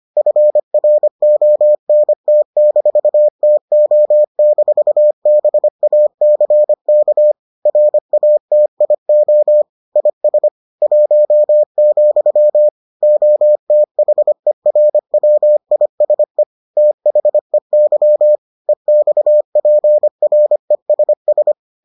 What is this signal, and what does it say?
Morse code, heard as a keyed tone, about 25 wpm, 600 Hz. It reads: FRONT-TO-BACK RATIO IS 1, OTHERWISE THEY EXPRESS